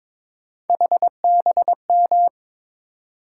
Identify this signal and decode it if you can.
Morse code, heard as a keyed tone, about 23 wpm, 700 Hz. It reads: HBM
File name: practice / 22wpm / Three Letter-Number Combinations - 22wpm